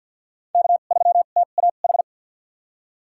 Morse code, heard as a keyed tone, about 48 wpm, spaced as for 35 wpm, 700 Hz. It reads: K3TAH
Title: USA Call Signs - 5 Characters - 2x Character Spacing - 50wpm